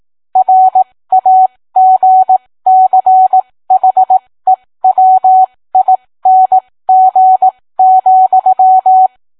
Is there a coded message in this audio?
Yes: Morse code from 0.4 to 9.1 s, heard as a keyed tone, about 18 wpm, spaced as for 15 wpm, 750 Hz, with radio noise. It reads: RAGCHEWING,